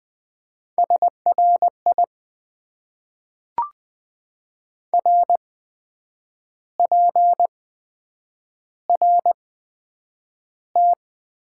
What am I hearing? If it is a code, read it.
Morse code, heard as a keyed tone, about 20 wpm, spaced as for 8 wpm, 700 Hz. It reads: SRI RPRT
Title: Top 90 QSO Elements - 8x Character Spacing - 20wpm